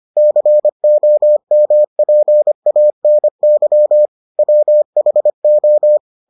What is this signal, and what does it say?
Morse code, heard as a keyed tone, about 25 wpm, 600 Hz. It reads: COMPANY WHO